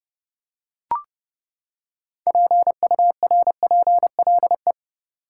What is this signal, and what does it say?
Morse code, heard as a keyed tone, about 30 wpm, 700 Hz. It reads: PURPLE
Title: Six Letter Words - 30wpm